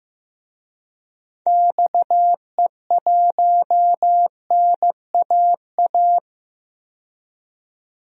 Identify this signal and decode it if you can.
Morse code, heard as a keyed tone, about 15 wpm, 700 Hz. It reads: XE1NAA
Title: DXCC-100 Call Signs - 6 Characters - 15wpm